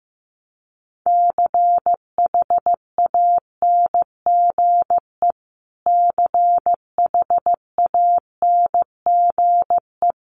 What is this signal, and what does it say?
Morse code, heard as a keyed tone, about 14 wpm, 700 Hz. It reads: CHANGE CHANGE
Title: Top 500 Words Repeated 15wpm